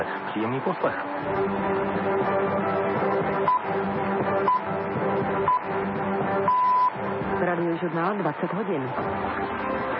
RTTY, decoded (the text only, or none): none